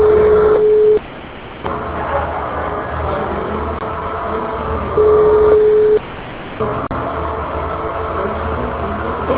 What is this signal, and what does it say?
Ringback tone (Europe, 425 Hz), with noise.